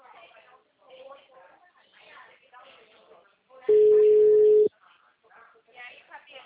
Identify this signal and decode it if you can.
Ringback tone (Europe, 425 Hz), with speech.